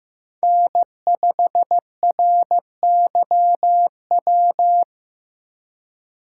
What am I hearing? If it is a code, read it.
Morse code, heard as a keyed tone, about 15 wpm, 700 Hz. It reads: N5RYW
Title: USA Call Signs - 6x Character Spacing - 15wpm